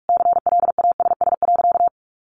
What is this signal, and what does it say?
Morse code, heard as a keyed tone, about 44 wpm, 700 Hz. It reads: CLASS.
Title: Sentences from Top 1400 Words 45wpm